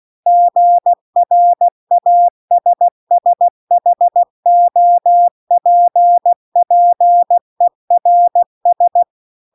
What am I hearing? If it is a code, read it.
Morse code, heard as a keyed tone, about 16 wpm, 700 Hz. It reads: GRASSHOPPERS